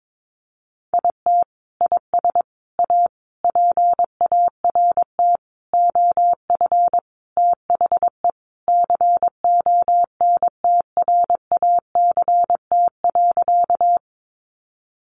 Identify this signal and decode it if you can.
Morse code, heard as a keyed tone, about 22 wpm, 700 Hz. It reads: IT IS A PART OF THE CONTRACT.
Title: Sentences from Top 1400 Words 22wpm